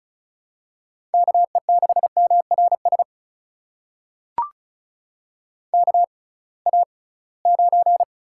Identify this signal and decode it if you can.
Morse code, heard as a keyed tone, about 36 wpm, spaced as for 23 wpm, 700 Hz. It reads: KE6MRS KA9
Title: USA Call Signs - 6 Characters - 6x Character Spacing - 35wpm